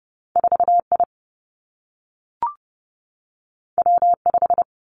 Morse code, heard as a keyed tone, about 30 wpm, 700 Hz. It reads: 4 I W 5
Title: Two Challenging Letter-Number Combinations 30wpm